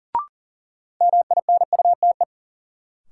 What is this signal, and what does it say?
Morse code, heard as a keyed tone, about 40 wpm, 700 Hz. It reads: MINUTE
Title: Sentences from Top 500 Words - New Word Drill - 40wpm